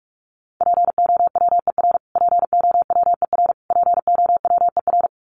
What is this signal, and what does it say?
Morse code, heard as a keyed tone, about 44 wpm, 700 Hz. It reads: POWER POWER POWER
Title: Top 500 Words Repeated 45wpm